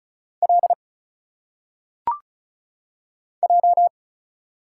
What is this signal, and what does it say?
Morse code, heard as a keyed tone, about 36 wpm, 700 Hz. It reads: L J
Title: Single Letters - Phonetic Pronunciation 35wpm